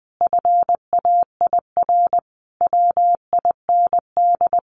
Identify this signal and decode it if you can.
Morse code, heard as a keyed tone, about 20 wpm, 700 Hz. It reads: FAIR WIND
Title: Sentences from Top 1400 Words 20wpm